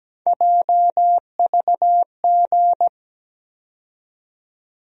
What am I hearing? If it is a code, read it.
Morse code, heard as a keyed tone, about 17 wpm, 700 Hz. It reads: JVG
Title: USA Call Sign Postfixes - 3 Letters 17wpm